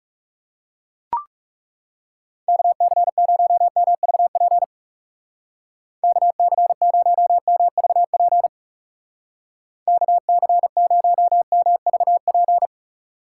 Morse code, heard as a keyed tone, about 38 wpm, 700 Hz. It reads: KC0MVP KC0MVP KC0MVP
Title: Speed Racing - USA Call Signs 30wpm